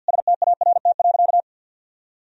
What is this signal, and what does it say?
Morse code, heard as a keyed tone, about 47 wpm, 700 Hz. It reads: START.
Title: Sentences from Top 1000 Words - Review Part 1 - 50wpm